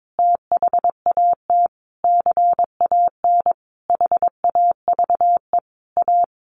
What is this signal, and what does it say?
Morse code, heard as a keyed tone, about 22 wpm, 700 Hz. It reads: THAT CAN HAVE A
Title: Sentences from Top 1300 Words 22wpm